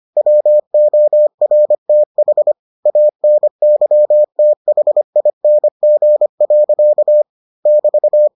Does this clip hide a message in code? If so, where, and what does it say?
Yes: Morse code from 0.2 to 8.3 s, heard as a keyed tone, about 25 wpm, 600 Hz. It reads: WORTH ANYTHING. <BT>